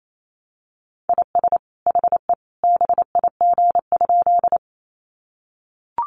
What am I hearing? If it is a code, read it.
Morse code, heard as a keyed tone, about 28 wpm, 700 Hz. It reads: IS HE BIG?
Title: Sentences from Top 200 Words 28wpm